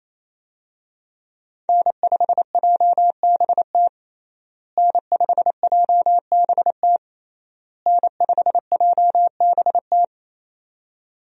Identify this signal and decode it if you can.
Morse code, heard as a keyed tone, about 28 wpm, 700 Hz. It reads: N5JBT N5JBT N5JBT
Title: USA Call Signs - 5 Characters - Repeated 3x - 28wpm